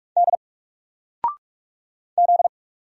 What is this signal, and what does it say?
Morse code, heard as a keyed tone, about 44 wpm, 700 Hz. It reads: D Z
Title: Single Character (Letter:Number:Punctuation:Prosign) 45wpm